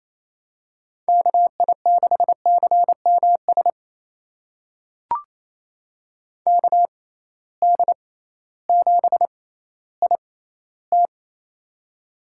Morse code, heard as a keyed tone, about 28 wpm, spaced as for 15 wpm, 700 Hz. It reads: KI6CMS KD7IT
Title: USA Call Signs - 6x Character Spacing - 28wpm